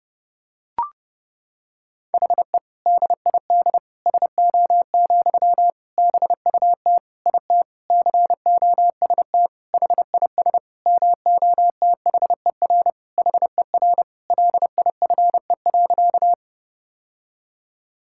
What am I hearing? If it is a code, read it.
Morse code, heard as a keyed tone, about 30 wpm, 700 Hz. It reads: HE DID SO, BUT IT COST HIS MOTHER HER LIFE.
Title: Sentences from Top 1000 Words - Review Part 5 - 30wpm